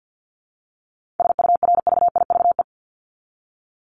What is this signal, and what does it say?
Morse code, heard as a keyed tone, about 50 wpm, 700 Hz. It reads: SURVIVE
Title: Top 1251-1500 Words 50wpm